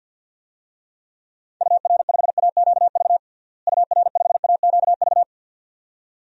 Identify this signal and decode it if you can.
Morse code, heard as a keyed tone, about 51 wpm, 700 Hz. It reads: UR5AQV UR5AQV
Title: DXCC-100 Call Signs - 6 Characters - Repeated 2x - 50wpm